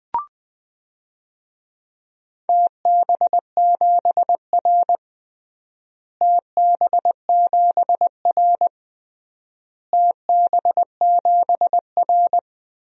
Morse code, heard as a keyed tone, about 20 wpm, 690 Hz. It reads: TB7R TB7R TB7R
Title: DXCC-100 Call Signs - 4 Characters - Repeated 3x - 20wpm